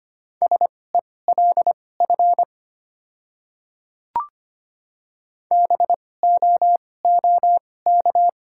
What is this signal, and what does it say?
Morse code, heard as a keyed tone, about 25 wpm, spaced as for 18 wpm, 690 Hz. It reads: SELF BOOK
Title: 2x Character Spacing - Top 300 Words - 25wpm